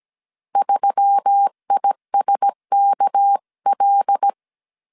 Morse code, heard as a keyed tone, about 17 wpm, 800 Hz. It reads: 3ISKL
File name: 16wpm